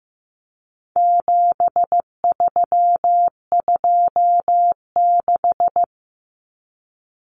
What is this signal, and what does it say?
Morse code, heard as a keyed tone, about 15 wpm, 700 Hz. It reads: 7326
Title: Common Cut Numbers - 4-Digit 15wpm